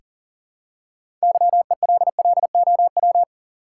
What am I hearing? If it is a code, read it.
Morse code, heard as a keyed tone, about 38 wpm, 700 Hz. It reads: YELLOW
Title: Six Letter Words - 40wpm